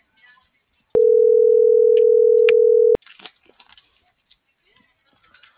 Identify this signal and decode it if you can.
Ringback tone (North America), 440 + 480 Hz.